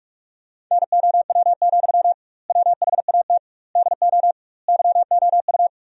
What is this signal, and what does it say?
Morse code, heard as a keyed tone, about 46 wpm, 700 Hz. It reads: NOW, WHAT DO YOU
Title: 2x Word-Spacing - Sentences from Top 200 Words - 45wpm